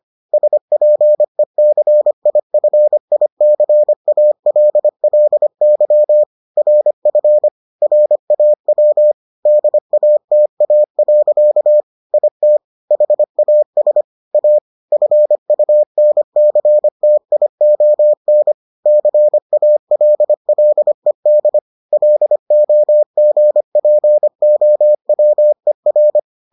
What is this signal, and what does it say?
Morse code, heard as a keyed tone, about 25 wpm, 600 Hz. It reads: SPECIFICALLY RF RAW DATA. IT HAS A FUNCTION CALLED LOGPOWER